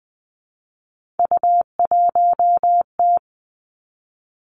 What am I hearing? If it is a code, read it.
Morse code, heard as a keyed tone, about 20 wpm, 700 Hz. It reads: U1T
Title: Three Challenging Letter-Number Combinations 20wpm